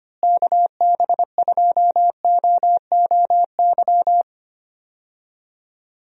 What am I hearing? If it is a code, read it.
Morse code, heard as a keyed tone, about 25 wpm, 700 Hz. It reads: KB2OOY
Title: USA Call Signs - 6 Characters - Repeated 2x - 25wpm